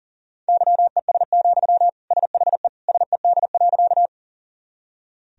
Morse code, heard as a keyed tone, about 40 wpm, 700 Hz. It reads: YES, SHE SED.
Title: Sentences from Top 500 Words and Abbreviations 40wpm